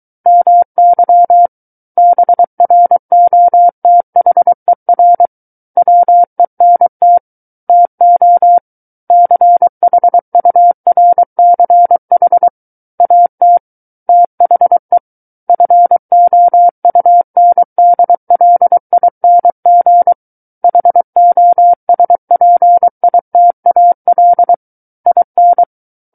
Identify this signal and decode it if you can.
Morse code, heard as a keyed tone, about 23 wpm, 700 Hz. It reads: MY BROTHER WENT TO CHURCH AT THE FOUNDLING HOSPITAL IN
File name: Book One / 14-Chapter 14 - 23 WPM